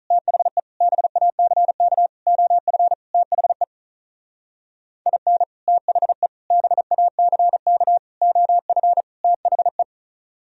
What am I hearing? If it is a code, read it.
Morse code, heard as a keyed tone, about 44 wpm, 700 Hz. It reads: THE BACK OF THE IN THE BACK OF THE